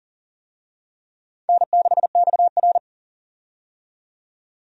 Morse code, heard as a keyed tone, about 38 wpm, 700 Hz. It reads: N6XR
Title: USA Call Signs - 4 Characters - 40wpm